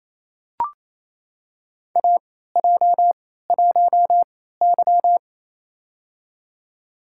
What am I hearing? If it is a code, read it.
Morse code, heard as a keyed tone, about 28 wpm, spaced as for 16 wpm, 700 Hz. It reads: AJ1Y
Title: USA Call Signs - 4 Characters - 3x Character Spacing - 28wpm